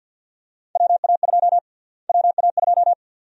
Morse code, heard as a keyed tone, about 52 wpm, 700 Hz. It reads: WA2 WA2
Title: USA Call Sign Prefixes - Repeated 2x - 50wpm